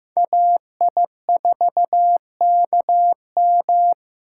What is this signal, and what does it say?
Morse code, heard as a keyed tone, about 15 wpm, 700 Hz. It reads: AI4KM
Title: USA Call Signs - 3x Character Spacing - 15wpm